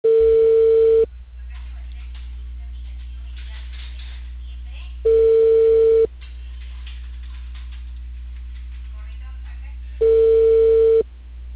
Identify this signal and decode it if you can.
Ringback tone (China, 450 Hz), with speech.